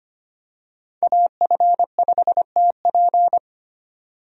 Morse code, heard as a keyed tone, about 25 wpm, 700 Hz. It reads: AF5TP
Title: USA Call Signs - 5 Characters - 25wpm